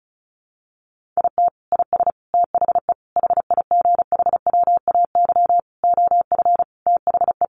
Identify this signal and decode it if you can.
Morse code, heard as a keyed tone, about 34 wpm, 700 Hz. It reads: IT IS THE HIGHWAY OF THE
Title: Sentences from Top 1400 Words 35wpm